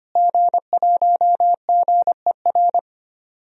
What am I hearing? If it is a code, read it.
Morse code, heard as a keyed tone, about 25 wpm, 700 Hz. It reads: G1GER
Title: DXCC-100 Call Signs - 5 Characters - Repeated 3x - 25wpm